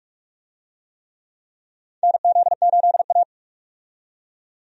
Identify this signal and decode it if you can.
Morse code, heard as a keyed tone, about 42 wpm, 700 Hz. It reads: NZ8A
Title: USA Call Signs - 4 Characters - 45wpm